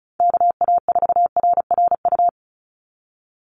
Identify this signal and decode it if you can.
Morse code, heard as a keyed tone, about 34 wpm, 700 Hz. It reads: KA4RRU
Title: Call Signs Repeated 3x - 35wpm